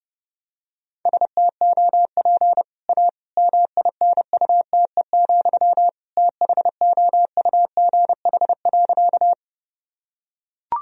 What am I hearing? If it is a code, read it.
Morse code, heard as a keyed tone, about 30 wpm, 700 Hz. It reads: STOP A MINUTE, THOUGH.
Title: Sentences from Top 1000 Words - Review Part 2 - 30wpm